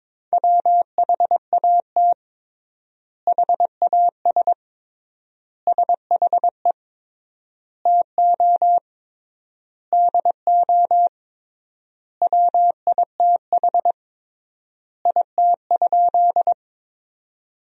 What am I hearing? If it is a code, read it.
Morse code, heard as a keyed tone, about 22 wpm, 700 Hz. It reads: WHAT HAS SHE TO DO WITH IT?
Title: 3x Word-Spacing - Sentences from Top 100 Words - 22wpm